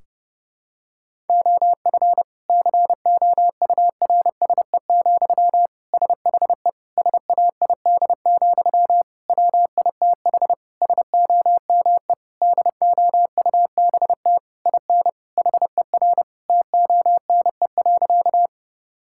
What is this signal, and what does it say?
Morse code, heard as a keyed tone, about 30 wpm, 700 Hz. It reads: OF COURSE, SHE SAID, WITH SOME DOUBT IN HER TONE.
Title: Sentences from Top 1600 Words 30wpm